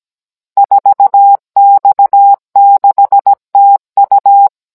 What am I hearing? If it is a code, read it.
Morse code, heard as a keyed tone, about 17 wpm, 800 Hz. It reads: 4X6TU
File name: Cours-NR18-16mm